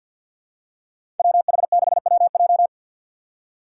Morse code, heard as a keyed tone, about 53 wpm, 700 Hz. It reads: WH6WJ
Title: USA Call Signs - 2x Character Spacing - 50wpm